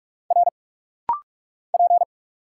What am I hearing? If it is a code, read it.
Morse code, heard as a keyed tone, about 44 wpm, 700 Hz. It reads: R P
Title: Single Letter-Number Alt - Rapid-Fire 45wpm